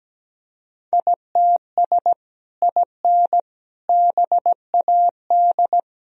Morse code, heard as a keyed tone, about 17 wpm, 700 Hz. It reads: ITS IN BAD
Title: Sentences from Top 1000 Words - Review Part 1 - 17wpm